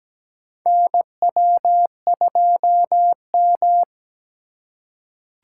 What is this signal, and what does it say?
Morse code, heard as a keyed tone, about 17 wpm, 700 Hz. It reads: NW2M
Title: USA Call Signs - 4 Characters - 2x Character Spacing - 17wpm